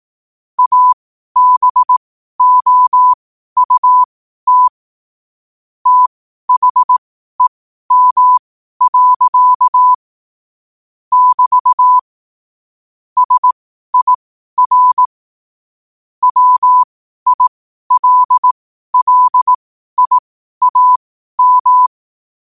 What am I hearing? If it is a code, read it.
Morse code, heard as a keyed tone, about 18 wpm, spaced as for 12 wpm, 1000 Hz. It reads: ABOUT THEM. <BT> SIR WILLIAM
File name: qotdfiles / 12wpm